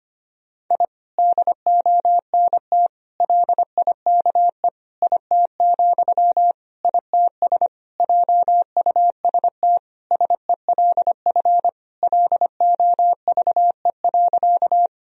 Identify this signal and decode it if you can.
Morse code, heard as a keyed tone, about 25 wpm, 700 Hz. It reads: I DONT LIKE IT, ITS JUST SELF LOVE.
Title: Sentences from Top 1000 Words - Review Part 1 - 25wpm